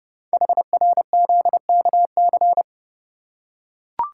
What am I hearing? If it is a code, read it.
Morse code, heard as a keyed tone, about 30 wpm, 700 Hz. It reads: HRZKC
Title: Code Groups - 5 Letters - 30wpm